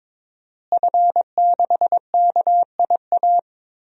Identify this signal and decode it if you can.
Morse code, heard as a keyed tone, about 22 wpm, 700 Hz. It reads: F6KIA
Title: DXCC-100 Call Signs - 5 Characters - Repeated 3x - 22wpm